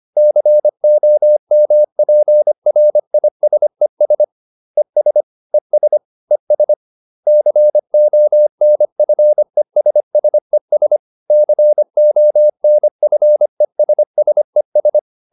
Morse code, heard as a keyed tone, about 25 wpm, 600 Hz. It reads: COMPRISES ES ES ES CONFESSES CONFESSES